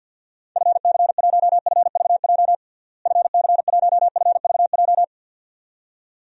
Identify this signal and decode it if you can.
Morse code, heard as a keyed tone, about 52 wpm, 700 Hz. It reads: FC1FVJ FC1FVJ
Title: DXCC-100 Call Signs - 6 Characters - Repeated 2x - 50wpm